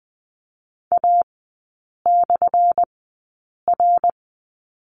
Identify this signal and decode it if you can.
Morse code, heard as a keyed tone, about 20 wpm, spaced as for 8 wpm, 690 Hz. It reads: A/R
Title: 212 Lesson 34e - Number 8 - Three Characters 20wpm